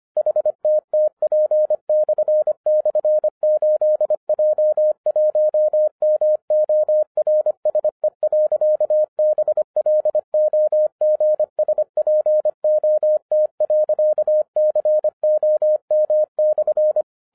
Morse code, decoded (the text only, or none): HTTP//8J1MORSE.BLOGSPOT.COM/